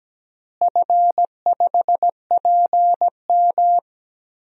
Morse code, heard as a keyed tone, about 17 wpm, 700 Hz. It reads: F5PM
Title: DXCC-100 Call Signs - 4 Characters - Repeated 3x - 17wpm